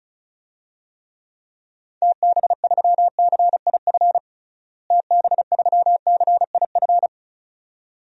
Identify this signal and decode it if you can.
Morse code, heard as a keyed tone, about 34 wpm, 700 Hz. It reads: TB3CIF TB3CIF
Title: DXCC-100 Call Signs - 6 Characters - Repeated 2x - 35wpm